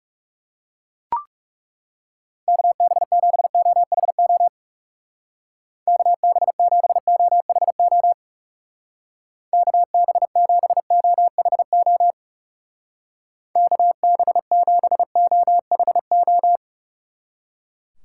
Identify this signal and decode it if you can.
Morse code, heard as a keyed tone, about 35 wpm, 700 Hz. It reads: KB7OHO KB7OHO KB7OHO KB7OHO
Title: Speed Racing - USA 2x3 Call Signs 30wpm